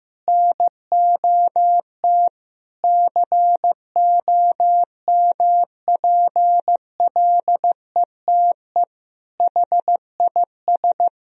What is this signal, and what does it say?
Morse code, heard as a keyed tone, about 15 wpm, 700 Hz. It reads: NOT COMPLETE HIS
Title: Sentences from Top 1000 Words - Review Part 1 - 15wpm